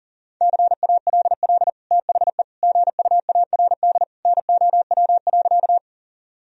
Morse code, heard as a keyed tone, about 40 wpm, 700 Hz. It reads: CALL THE GUARD NOW.
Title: Sentences from Top 1000 Words - Review Part 4 - 40wpm